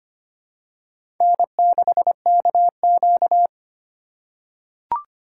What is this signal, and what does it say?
Morse code, heard as a keyed tone, about 25 wpm, 700 Hz. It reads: N6KQ
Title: USA Call Signs - 4 Characters - 2x Character Spacing - 25wpm